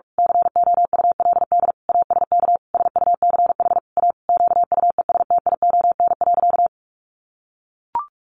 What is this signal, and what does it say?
Morse code, heard as a keyed tone, about 44 wpm, 700 Hz. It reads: COULD ASK SUCH A QUESTION.
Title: Sentences from Top 600 Words 45wpm